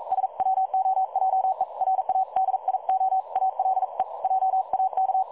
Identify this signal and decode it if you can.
Morse code, heard as a keyed tone, about 42 wpm, 760 Hz. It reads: SWOJE RADIO ALE JAK
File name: dlazaawansowanych / nagraniazpasma / websdr recording 3565.0kHz